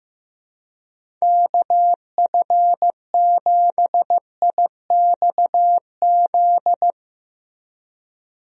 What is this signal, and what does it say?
Morse code, heard as a keyed tone, about 16 wpm, 700 Hz. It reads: KF7IXZ
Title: USA Call Signs - 6 Characters - Repeated 2x - 15wpm